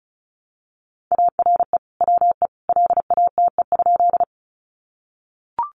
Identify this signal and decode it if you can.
Morse code, heard as a keyed tone, about 34 wpm, 700 Hz. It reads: ARE WE LATE?
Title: Sentences from Top 300 Words 35wpm